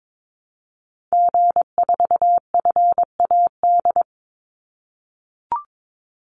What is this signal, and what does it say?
Morse code, heard as a keyed tone, about 22 wpm, 700 Hz. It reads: G4FAD
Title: Call Signs - 22wpm